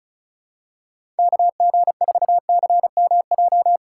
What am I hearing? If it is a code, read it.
Morse code, heard as a keyed tone, about 36 wpm, 700 Hz. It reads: KG4CMJ
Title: USA Call Signs - 4x Character Spacing - 35wpm